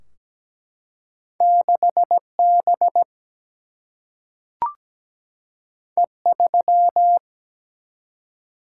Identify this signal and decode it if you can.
Morse code, heard as a keyed tone, about 17 wpm, 700 Hz. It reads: 6 B E 3
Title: Single-Letter Single-Number 17wpm